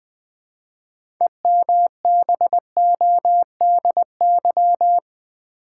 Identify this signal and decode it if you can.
Morse code, heard as a keyed tone, about 20 wpm, 700 Hz. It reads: EMBODY
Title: 6 Letter Words - First 3 Letters Given - 20wpm